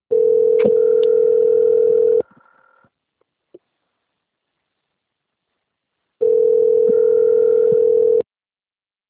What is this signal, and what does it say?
Ringback tone (North America), 440 + 480 Hz.